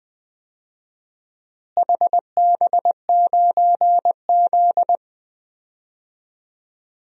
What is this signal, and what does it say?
Morse code, heard as a keyed tone, about 20 wpm, 700 Hz. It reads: HB9Z
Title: DXCC-100 Call Signs - 4 Characters - Repeated 3x - 20wpm